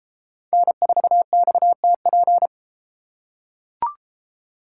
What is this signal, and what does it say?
Morse code, heard as a keyed tone, about 31 wpm, 700 Hz. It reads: N4XTP